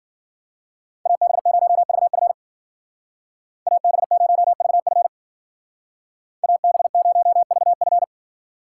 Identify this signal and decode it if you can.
Morse code, heard as a keyed tone, about 52 wpm, 700 Hz. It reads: AB0VF AB0VF AB0VF